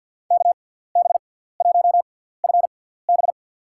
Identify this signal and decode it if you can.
Morse code, heard as a keyed tone, about 51 wpm, spaced as for 17 wpm, 690 Hz. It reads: KB1FB
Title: USA Call Signs - 5 Characters - 6x Character Spacing - 50wpm